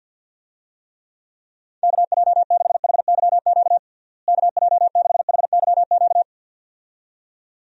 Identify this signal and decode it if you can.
Morse code, heard as a keyed tone, about 52 wpm, 700 Hz. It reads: KJ6HYQ KJ6HYQ
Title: USA Call Signs - 6 Characters - Repeated 2x - 50wpm